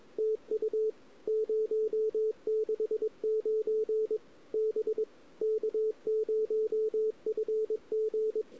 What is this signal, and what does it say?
Morse code, heard as a keyed tone, about 23 wpm, 410 Hz. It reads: TU 069 B K0FG
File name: CW / 2008-ss-cw-k5tr-op-n5rz -01:00